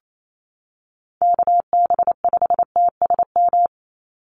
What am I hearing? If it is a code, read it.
Morse code, heard as a keyed tone, about 28 wpm, 700 Hz. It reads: KB5TSM
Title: USA Call Signs 28wpm